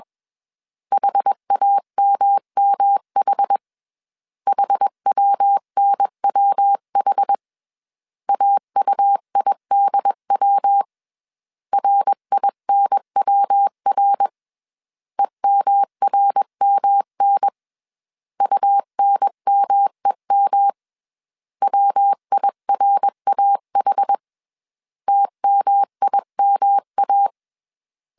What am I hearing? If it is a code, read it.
Morse code, heard as a keyed tone, about 21 wpm, 800 Hz. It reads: HAMMH HWNWH AUIDW RINWR EMRMN UNMEM WIRAH TMIMA
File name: CW / DET / 1 3 1500C S B 20wpm 1754